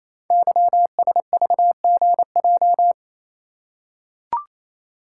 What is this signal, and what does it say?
Morse code, heard as a keyed tone, about 29 wpm, 700 Hz. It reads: YSVGJ
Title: Code Groups - 5 Letters - 28wpm